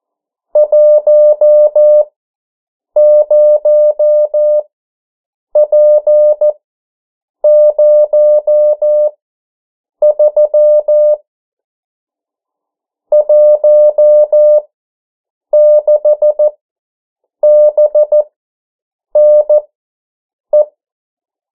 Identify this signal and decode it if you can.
Morse code, heard as a keyed tone, about 14 wpm, spaced as for 7 wpm, 600 Hz. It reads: 10P03 16BNE
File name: morse mp3 / 09 urok 8